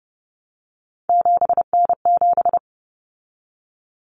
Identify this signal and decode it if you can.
Morse code, heard as a keyed tone, about 30 wpm, 700 Hz. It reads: 7N7
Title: Common Cut Numbers - 3-Digit 30wpm